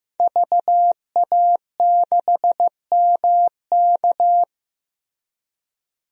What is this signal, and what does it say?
Morse code, heard as a keyed tone, about 15 wpm, 700 Hz. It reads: VA6MK
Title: DXCC-100 Call Signs - 5 Characters - 15wpm